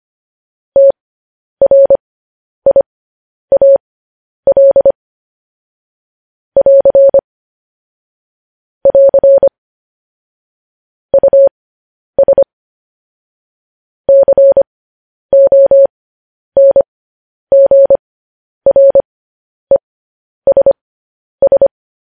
Morse code, heard as a keyed tone, about 25 wpm, spaced as for 10 wpm, 550 Hz. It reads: TRIAL <AR> <AR> US CONGRESS
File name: podcast / News headlines-1x-25x10